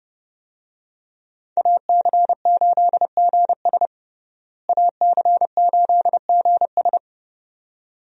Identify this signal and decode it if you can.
Morse code, heard as a keyed tone, about 30 wpm, 700 Hz. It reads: AC8GS AC8GS
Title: USA Call Signs - 5 Characters - Repeated 2x - 30wpm